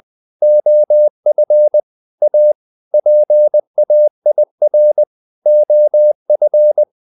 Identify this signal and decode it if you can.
Morse code, heard as a keyed tone, about 20 wpm, 600 Hz. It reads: OF A PAIR OF
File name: the old man and the sea / the old man and the sea part 1 20wpm